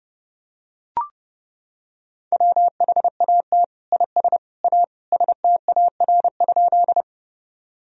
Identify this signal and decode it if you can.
Morse code, heard as a keyed tone, about 30 wpm, 700 Hz. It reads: WHAT IS A STAR?
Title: Sentences from Top 1000 Words - Review Part 2 - 30wpm